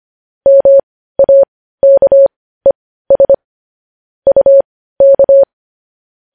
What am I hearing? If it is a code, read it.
Morse code, heard as a keyed tone, about 25 wpm, spaced as for 15 wpm, 550 Hz. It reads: MAKES UK
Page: News Headlines in Morse Code at 15 WPM